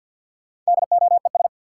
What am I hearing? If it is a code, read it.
Morse code, heard as a keyed tone, about 51 wpm, 700 Hz. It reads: DOES
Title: Sentences from Top 200 Words - New Word Drill - 50wpm